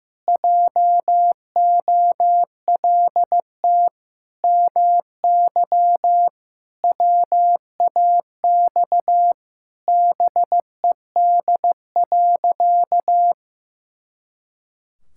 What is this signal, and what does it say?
Morse code, heard as a keyed tone, about 15 wpm, 700 Hz. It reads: JOLT MY WAX BED.
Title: Pangram Challenge v2 - 15wpm